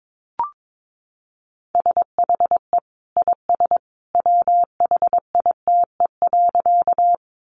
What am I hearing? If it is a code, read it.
Morse code, heard as a keyed tone, about 22 wpm, 700 Hz. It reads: SHE IS WHITE.
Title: Easy Phrases 22wpm